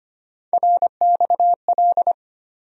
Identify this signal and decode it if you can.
Morse code, heard as a keyed tone, about 25 wpm, 700 Hz. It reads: RXL
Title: USA Call Sign Postfixes - 3 Letters 25wpm